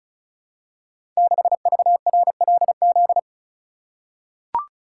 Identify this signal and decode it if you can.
Morse code, heard as a keyed tone, about 36 wpm, 700 Hz. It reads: 6VRLZ
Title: Code Groups - 5 Letter:Numbers - 35wpm